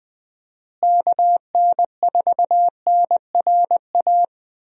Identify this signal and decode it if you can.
Morse code, heard as a keyed tone, about 21 wpm, 700 Hz. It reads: KN4NRA
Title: USA Call Signs - 6 Characters - Repeated 3x - 20wpm